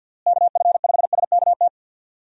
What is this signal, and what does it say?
Morse code, heard as a keyed tone, about 48 wpm, 700 Hz. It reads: KF5SKT
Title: USA Call Signs - 6 Characters - Repeated 2x - 50wpm